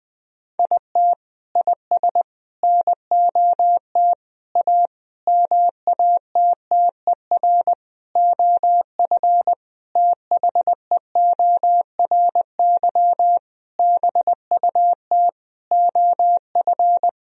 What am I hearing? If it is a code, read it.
Morse code, heard as a keyed tone, about 20 wpm, 700 Hz. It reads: IT IS NOT A MATTER OF THEORY BUT OF
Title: Sentences from Top 1000 Words - Review Part 4 - 20wpm